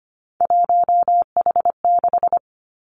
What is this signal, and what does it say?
Morse code, heard as a keyed tone, about 25 wpm, 700 Hz. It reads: 1H6
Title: Three Challenging Letter-Number Combinations 25wpm